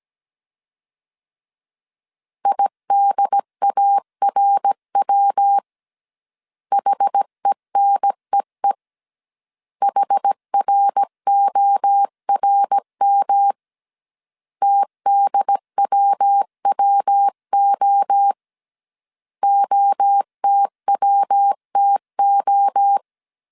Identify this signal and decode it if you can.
Morse code, heard as a keyed tone, about 17 wpm, 800 Hz. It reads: IDARW HENEE HRORM TDWWO OTWTO
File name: CW / DET / 1 3 1500C S A 16wpm 2141